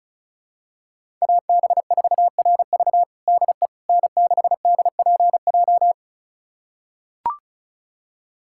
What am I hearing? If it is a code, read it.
Morse code, heard as a keyed tone, about 36 wpm, 700 Hz. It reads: AB4RV DE N6DPJ
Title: USA Callsign DE Callsign - 5 Characters - 35wpm